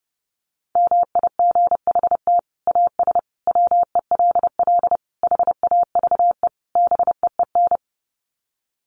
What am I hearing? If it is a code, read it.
Morse code, heard as a keyed tone, about 30 wpm, 700 Hz. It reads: MIGHT AS WELL HAVE BEEN
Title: Sets of 5 Words Encore2 30wpm